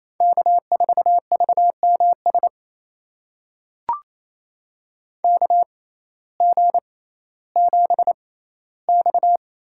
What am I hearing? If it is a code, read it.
Morse code, heard as a keyed tone, about 28 wpm, spaced as for 15 wpm, 700 Hz. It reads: K4VMS KG7X
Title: USA Call Signs - 5 Characters - 6x Character Spacing - 28wpm